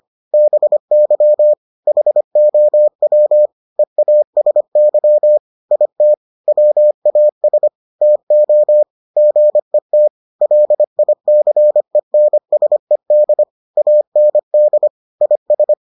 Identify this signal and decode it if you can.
Morse code, heard as a keyed tone, about 25 wpm, 600 Hz. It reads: BY HOW EASY IT WAS TO GET LICENSED AND IS